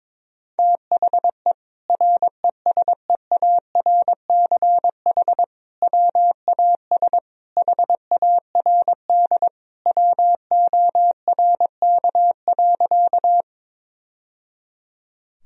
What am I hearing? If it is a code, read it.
Morse code, heard as a keyed tone, about 22 wpm, 700 Hz. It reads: THE RESEARCH WAS HARD WORK.
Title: Sentences from Top 1000 Words - Review Part 4 - 22wpm